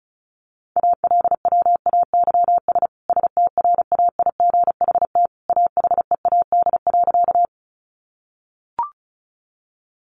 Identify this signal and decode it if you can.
Morse code, heard as a keyed tone, about 34 wpm, 700 Hz. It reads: ALWAYS STRAIGHT AHEAD.
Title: Sentences from Top 1100 Words 35wpm